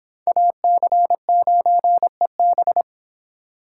Morse code, heard as a keyed tone, about 26 wpm, 700 Hz. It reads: AC9EB